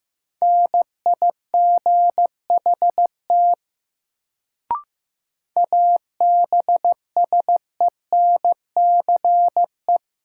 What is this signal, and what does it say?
Morse code, heard as a keyed tone, about 16 wpm, 700 Hz. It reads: NIGHT ABSENCE
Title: Sets of 4 Words - New Word Drill - 15wpm